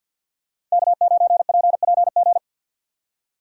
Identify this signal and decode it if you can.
Morse code, heard as a keyed tone, about 52 wpm, 700 Hz. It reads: K9PPG
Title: USA Call Signs - 5 Characters - 3x Character Spacing - 50wpm